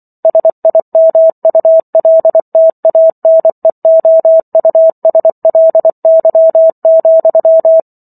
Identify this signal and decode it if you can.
Morse code, heard as a keyed tone, about 24 wpm, 650 Hz. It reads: SIMULTANEOUSLY,